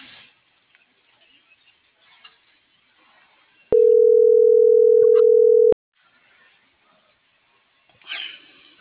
Ringback tone (North America), 440 + 480 Hz.